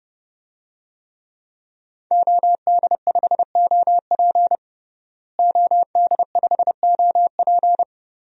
Morse code, heard as a keyed tone, about 30 wpm, 700 Hz. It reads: OD5OP OD5OP
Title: DXCC-100 Call Signs - 5 Characters - Repeated 2x - 30wpm